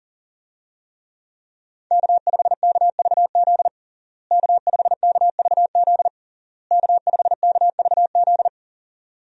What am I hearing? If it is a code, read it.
Morse code, heard as a keyed tone, about 40 wpm, 700 Hz. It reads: K5KVZ K5KVZ K5KVZ
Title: USA Call Signs - 5 Characters - Repeated 3x - 40wpm